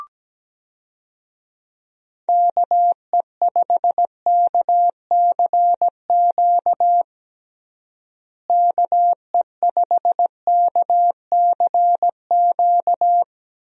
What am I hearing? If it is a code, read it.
Morse code, heard as a keyed tone, about 17 wpm, 700 Hz. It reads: KE5KCQ KE5KCQ
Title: USA Call Signs - 6 Characters - Repeated 2x - 17wpm